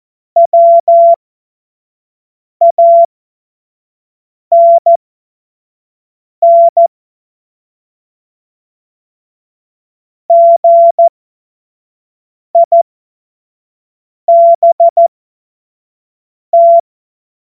Morse code, heard as a keyed tone, about 14 wpm, spaced as for 5 wpm, 680 Hz. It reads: WANN GIBT
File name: Morsekurs / Witze / Witz 25. Mai 2021, 25 F70